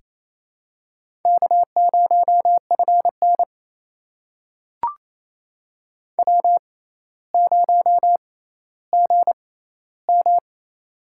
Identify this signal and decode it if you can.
Morse code, heard as a keyed tone, about 28 wpm, spaced as for 14 wpm, 700 Hz. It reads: K0FN W0GM
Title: USA Call Signs - 4 Characters - 6x Character Spacing - 28wpm